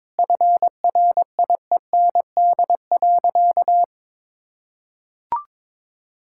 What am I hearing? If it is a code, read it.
Morse code, heard as a keyed tone, about 21 wpm, 700 Hz. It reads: FRIEND.
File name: practice / 22wpm / Sentences with 8-Word Phrases - 22wpm